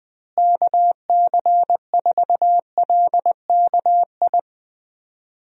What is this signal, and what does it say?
Morse code, heard as a keyed tone, about 20 wpm, 700 Hz. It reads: KC4LKI
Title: USA Call Signs - 6 Characters - Repeated 3x - 20wpm